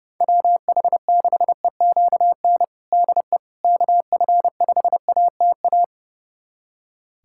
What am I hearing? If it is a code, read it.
Morse code, heard as a keyed tone, about 32 wpm, 700 Hz. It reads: WH6EQN DE KF5ATA
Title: USA Callsign DE Callsign - 6 Characters - 30wpm